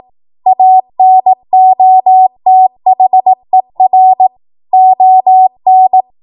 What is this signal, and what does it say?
Morse code, heard as a keyed tone, about 18 wpm, 750 Hz. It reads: ANOTHER ON
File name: Archive / 18 WPM / 18WPM